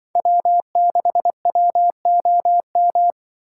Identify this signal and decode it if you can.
Morse code, heard as a keyed tone, about 24 wpm, 700 Hz. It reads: W6WOM
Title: Speed Racing - USA Call Signs 20wpm